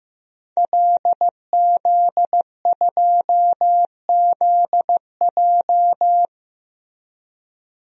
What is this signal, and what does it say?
Morse code, heard as a keyed tone, about 15 wpm, 700 Hz. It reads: LZ2ZJ
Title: DXCC-100 Call Signs - 5 Characters - Repeated 2x - 15wpm